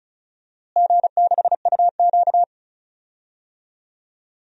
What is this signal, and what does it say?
Morse code, heard as a keyed tone, about 34 wpm, 700 Hz. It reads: G6UQ
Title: DXCC-100 Call Signs - 4 Characters - Repeated 3x - 35wpm